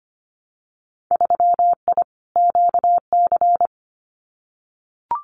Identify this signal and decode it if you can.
Morse code, heard as a keyed tone, about 25 wpm, 700 Hz. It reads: 3I QC
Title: Winter Field Day Exchanges Encore 25wpm